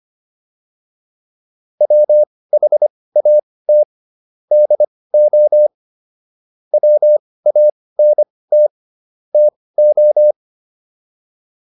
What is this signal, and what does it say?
Morse code, heard as a keyed tone, about 25 wpm, spaced as for 18 wpm, 600 Hz. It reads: WHAT DO WANT TO